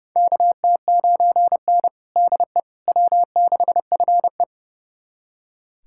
Morse code, heard as a keyed tone, about 32 wpm, 700 Hz. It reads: KT9N DE W6FE
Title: USA Callsign DE Callsign - 4 Characters - 30wpm